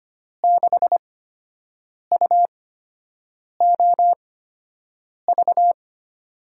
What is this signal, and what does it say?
Morse code, heard as a keyed tone, about 25 wpm, spaced as for 7 wpm, 700 Hz. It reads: 6UOV
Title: USA Call Signs - 6 Characters - 8x Character Spacing - 25wpm